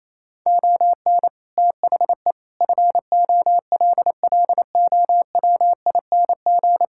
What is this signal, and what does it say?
Morse code, heard as a keyed tone, about 28 wpm, 700 Hz. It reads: ON THE FOLLOWING
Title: Sentences from Top 1000 Words - Review Part 6 - 28wpm